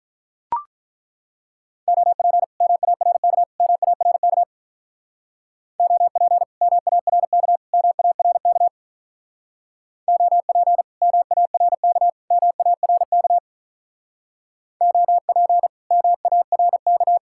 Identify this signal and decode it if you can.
Morse code, heard as a keyed tone, about 43 wpm, 700 Hz. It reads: OP MARK MARK OP MARK MARK OP MARK MARK OP MARK
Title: Speed-Racing Common QSO Phrases - 35wpm